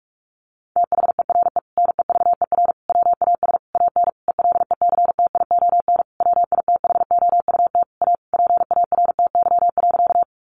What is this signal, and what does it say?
Morse code, heard as a keyed tone, about 44 wpm, 700 Hz. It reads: THERE NEVER WAS AN ELECTION WITHOUT A PARTY.
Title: Sentences from Top 1500 Words 45wpm